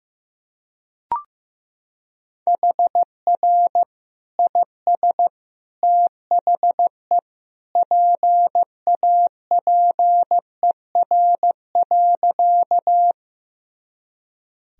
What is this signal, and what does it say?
Morse code, heard as a keyed tone, about 15 wpm, 700 Hz. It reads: HR IS THE PAPER.
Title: Sentences from Top 300 Words and Abbreviations 15wpm